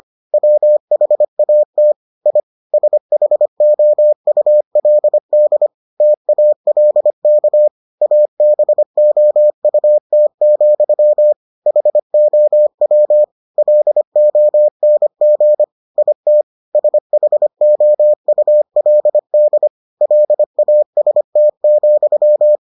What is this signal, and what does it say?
Morse code, heard as a keyed tone, about 25 wpm, 600 Hz. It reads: WHAT I SHOULD TALK ABOUT, HOW LONG IT SHOULD LAST,